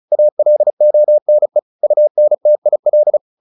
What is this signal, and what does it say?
Morse code, heard as a keyed tone, about 34 wpm, 600 Hz. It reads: ALONE UNTIL
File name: the old man and the sea / the old man and the sea 1 35wpm